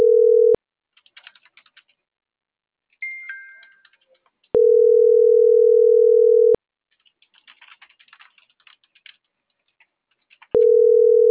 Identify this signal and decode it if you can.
Ringback tone (North America), 440 + 480 Hz.